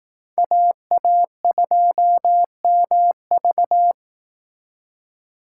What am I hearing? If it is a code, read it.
Morse code, heard as a keyed tone, about 18 wpm, 700 Hz. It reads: AA2MV
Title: Speed Racing - USA 2x2 Call Signs 15wpm